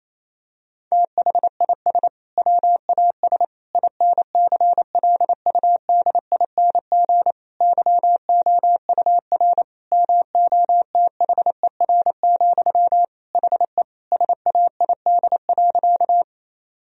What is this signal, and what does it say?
Morse code, heard as a keyed tone, about 28 wpm, 700 Hz. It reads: THIS WAS INCLUDING YOUR MOTHER, HE SAID.
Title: Sentences from Top 1000 Words - Review Part 4 - 28wpm